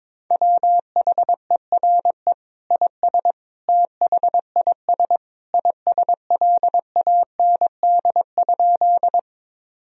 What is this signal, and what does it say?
Morse code, heard as a keyed tone, about 22 wpm, 700 Hz. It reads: WHERE IS THIS ISLAND?
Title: Sentences from Top 1000 Words - Review Part 3 - 22wpm